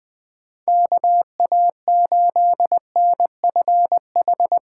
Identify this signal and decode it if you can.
Morse code, heard as a keyed tone, about 20 wpm, 700 Hz. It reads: KA8NFH
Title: USA Call Signs - 6 Characters - 8x Character Spacing - 20wpm